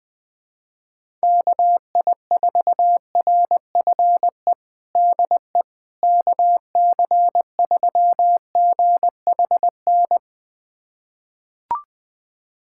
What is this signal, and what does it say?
Morse code, heard as a keyed tone, about 21 wpm, 700 Hz. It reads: KI4RFE DE KC3GHN
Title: USA Callsign DE Callsign - 6 Characters - 20wpm